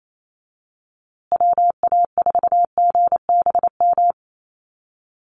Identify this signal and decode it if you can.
Morse code, heard as a keyed tone, about 28 wpm, 700 Hz. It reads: WA4GBM
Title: USA Call Signs Repeated 3x 28wpm